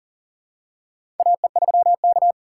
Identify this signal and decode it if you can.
Morse code, heard as a keyed tone, about 40 wpm, 700 Hz. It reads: AE3K
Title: USA Call Signs - 4x Character Spacing - 40wpm